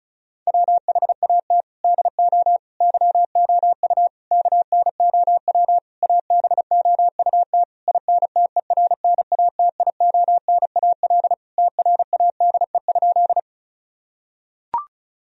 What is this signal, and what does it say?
Morse code, heard as a keyed tone, about 36 wpm, 700 Hz. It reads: WHAT DO YOU KNOW ABOUT INTERNATIONAL TRADE?
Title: Sentences from Top 1000 Words - Review Part 4 - 35wpm